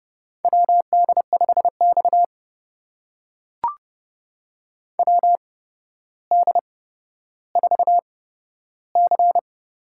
Morse code, heard as a keyed tone, about 30 wpm, spaced as for 13 wpm, 700 Hz. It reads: WD5X WD4C
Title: USA Call Signs - 4 Characters - 8x Character Spacing - 30wpm